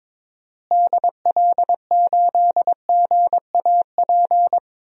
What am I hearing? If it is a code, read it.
Morse code, heard as a keyed tone, about 22 wpm, 700 Hz. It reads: DL8GAP
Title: DXCC-100 Call Signs - 6 Characters - Repeated 2x - 22wpm